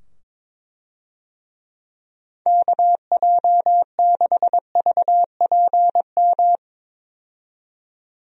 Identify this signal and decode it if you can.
Morse code, heard as a keyed tone, about 22 wpm, 700 Hz. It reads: KJ6VPM
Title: USA Call Signs - 6 Characters - 22wpm